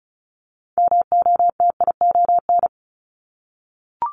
Morse code, heard as a keyed tone, about 36 wpm, 700 Hz. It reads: MOTION
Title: Multi-letter Phonemes 35wpm